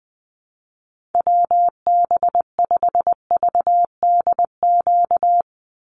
Morse code, heard as a keyed tone, about 20 wpm, 700 Hz. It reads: WB5VDQ
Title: USA Call Signs 20wpm